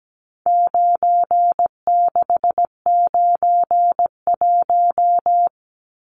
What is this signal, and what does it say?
Morse code, heard as a keyed tone, about 17 wpm, 700 Hz. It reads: 9691
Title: Numbers - 4-Digit 17wpm